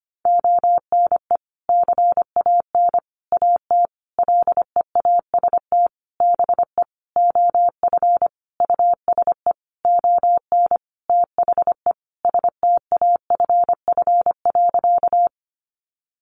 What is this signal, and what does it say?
Morse code, heard as a keyed tone, about 25 wpm, 700 Hz. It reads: ONE CAN AT LEAST BE OF USE ON THE STAFF.
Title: Sentences from Top 1200 Words 25wpm